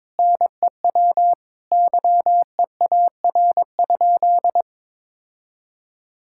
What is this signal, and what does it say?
Morse code, heard as a keyed tone, about 22 wpm, 700 Hz. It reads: NEW YEAR?
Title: Sentences from Top 1000 Words - Review Part 5 - 22wpm